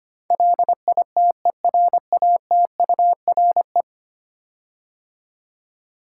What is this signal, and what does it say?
Morse code, heard as a keyed tone, about 25 wpm, 700 Hz. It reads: LITERATURE
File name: 10 Letter Words - First 3 Letters Given - 25wpm